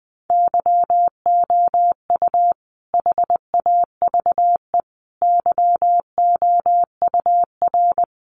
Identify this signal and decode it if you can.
Morse code, heard as a keyed tone, about 20 wpm, 700 Hz. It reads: YOU HAVE YOUR
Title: Sentences from Top 1400 Words 20wpm